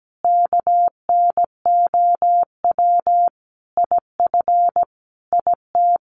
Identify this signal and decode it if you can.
Morse code, heard as a keyed tone, about 17 wpm, 700 Hz. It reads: KNOW IF IT
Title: Sets of 5 Words 17wpm